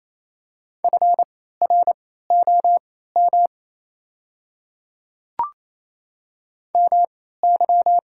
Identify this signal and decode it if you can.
Morse code, heard as a keyed tone, about 28 wpm, spaced as for 16 wpm, 690 Hz. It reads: FROM MY
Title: 3x Character Spacing - Top 100 Words - 28wpm